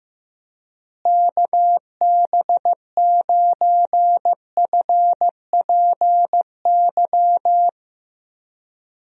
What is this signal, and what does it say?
Morse code, heard as a keyed tone, about 15 wpm, 700 Hz. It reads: KB9FPY
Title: USA Call Signs - 6 Characters - Repeated 3x - 15wpm